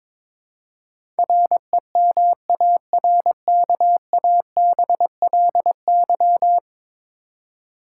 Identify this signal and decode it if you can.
Morse code, heard as a keyed tone, about 22 wpm, 700 Hz. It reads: REMARKABLY
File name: practice / 22wpm / 10 Letter Words - First 3 Letters Given - 22wpm